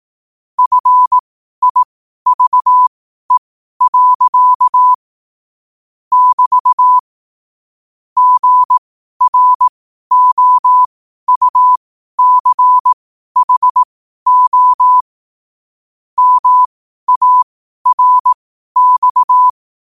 Morse code, heard as a keyed tone, about 18 wpm, spaced as for 12 wpm, 1000 Hz. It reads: FIVE. <BT> GROUCHO MARX
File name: qotdfiles / 12wpm